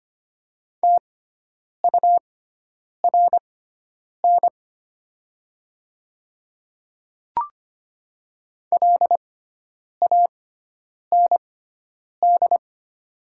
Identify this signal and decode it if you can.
Morse code, heard as a keyed tone, about 25 wpm, spaced as for 9 wpm, 700 Hz. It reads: TURN LAND
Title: 6x Character Spacing - Top 200 Words - 25wpm